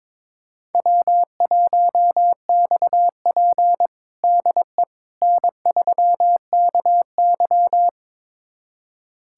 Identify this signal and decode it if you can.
Morse code, heard as a keyed tone, about 22 wpm, 700 Hz. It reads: W1XP DE N3KY
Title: USA Callsign DE Callsign - 4 Characters - 22wpm